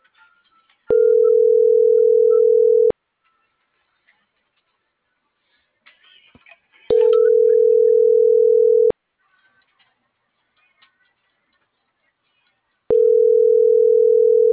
Ringback tone (North America), 440 + 480 Hz.